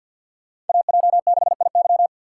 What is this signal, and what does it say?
Morse code, heard as a keyed tone, about 48 wpm, 700 Hz. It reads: AJ6IY
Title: USA Call Signs - 5 Characters - 6x Character Spacing - 50wpm